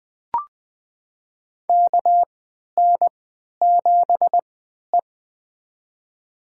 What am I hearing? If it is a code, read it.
Morse code, heard as a keyed tone, about 20 wpm, spaced as for 11 wpm, 700 Hz. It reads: KN7E
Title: USA Call Signs - 3x Character Spacing - 20wpm